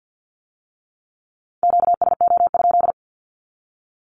Morse code, heard as a keyed tone, about 50 wpm, 700 Hz. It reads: QSO?
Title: Q-Codes with Spoken Meaning - 50wpm